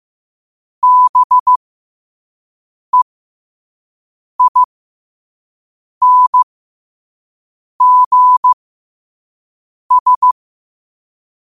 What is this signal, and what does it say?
Morse code, heard as a keyed tone, about 15 wpm, spaced as for 5 wpm, 1000 Hz. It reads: BEINGS
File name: qotdfiles / 5wpm